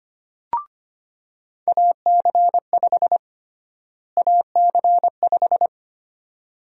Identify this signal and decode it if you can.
Morse code, heard as a keyed tone, about 25 wpm, 700 Hz. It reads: AC5 AC5
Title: USA Call Sign Prefixes - Repeated 2x - 25wpm